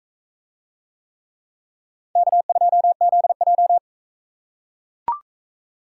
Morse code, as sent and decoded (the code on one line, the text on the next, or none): -.- ..--- --.. .---
K2ZJ